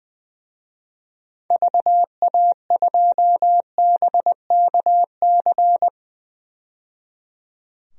Morse code, heard as a keyed tone, about 20 wpm, 700 Hz. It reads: VA2BKC
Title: DXCC-100 Call Signs - 6 Characters - 20wpm